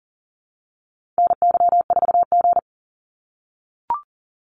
Morse code, heard as a keyed tone, about 40 wpm, 700 Hz. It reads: NY4G
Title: NA SOTA Activator Callsigns 40wpm